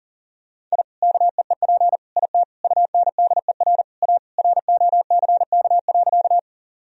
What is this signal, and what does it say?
Morse code, heard as a keyed tone, about 40 wpm, 700 Hz. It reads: I KEEP IT UNDER A ROCK.
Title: Sentences from Top 1000 Words - Review Part 2 - 40wpm